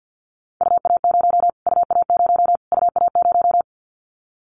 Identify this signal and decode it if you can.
Morse code, heard as a keyed tone, about 50 wpm, 700 Hz. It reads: UA0 UA0 UA0
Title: Callsign Prefixes for Top 25 DXCC - Repeated 3x - 50wpm